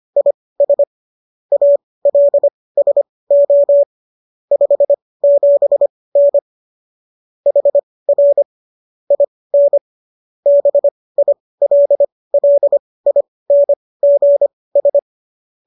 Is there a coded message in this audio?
Yes: Morse code from 0.2 to 15.0 s, heard as a keyed tone, about 25 wpm, spaced as for 18 wpm, 570 Hz. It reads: IS ALSO 57N HR IN BILLINGS